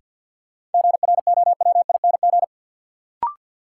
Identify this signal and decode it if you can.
Morse code, heard as a keyed tone, about 53 wpm, 700 Hz. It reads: GROWING
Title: Sets of 5 Words Encore - New Word Drill - 50wpm